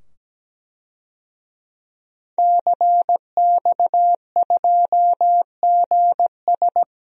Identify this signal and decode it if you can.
Morse code, heard as a keyed tone, about 17 wpm, 700 Hz. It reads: CX2GS
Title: DXCC-100 Call Signs - 5 Characters - 17wpm